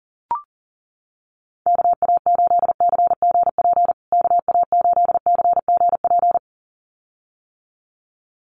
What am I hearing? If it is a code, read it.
Morse code, heard as a keyed tone, about 40 wpm, 700 Hz. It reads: KA8CGP KA8CGP
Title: USA Call Signs Repeated 2x 40wpm